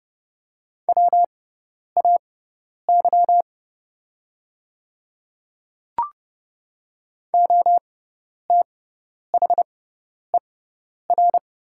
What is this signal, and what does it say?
Morse code, heard as a keyed tone, about 30 wpm, spaced as for 10 wpm, 700 Hz. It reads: WAY OTHER
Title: 6x Character Spacing - Top 100 Words - 30wpm